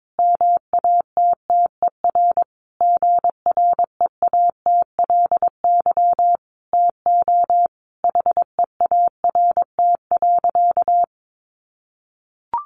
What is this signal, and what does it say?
Morse code, heard as a keyed tone, about 22 wpm, 700 Hz. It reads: MATTER GREATLY TO HEART.
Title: Sentences from Top 1400 Words 22wpm